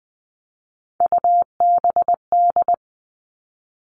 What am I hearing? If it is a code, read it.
Morse code, heard as a keyed tone, about 20 wpm, 700 Hz. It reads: UBD
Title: All Cut Numbers - 3-Digit 20wpm